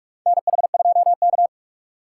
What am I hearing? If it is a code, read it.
Morse code, heard as a keyed tone, about 44 wpm, 700 Hz. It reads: NH2K
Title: USA Call Signs - 4 Characters - 8x Character Spacing - 45wpm